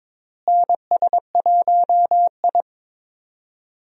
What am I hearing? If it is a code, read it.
Morse code, heard as a keyed tone, about 22 wpm, 700 Hz. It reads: NS1I
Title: USA Call Signs - 4 Characters - Repeated 3x - 22wpm